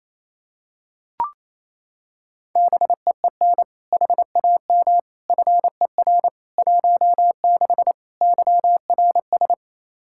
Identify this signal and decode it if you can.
Morse code, heard as a keyed tone, about 28 wpm, 700 Hz. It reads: BEEN HAM FER 16 YRS
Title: Practice QSO - Exchange 3 - More Info - 28wpm